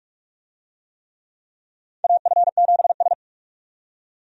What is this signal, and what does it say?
Morse code, heard as a keyed tone, about 41 wpm, 700 Hz. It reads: AF7S
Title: USA Call Signs - 4 Characters - 45wpm